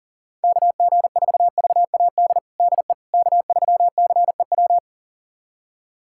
Morse code, heard as a keyed tone, about 40 wpm, 700 Hz. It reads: KG4VAD DE K3CEW
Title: USA Callsign DE Callsign 40wpm